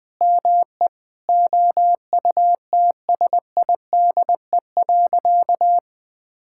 Morse code, heard as a keyed tone, about 20 wpm, 700 Hz. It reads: ME OUTSIDE.
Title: Sentences from Top 1000 Words - Review Part 6 - 20wpm